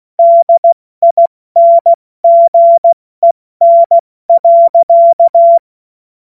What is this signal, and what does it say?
Morse code, heard as a keyed tone, about 16 wpm, 680 Hz. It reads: DINGEN.
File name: Morsekurs / Witze / Witz 04. Januar 2022, 70 F80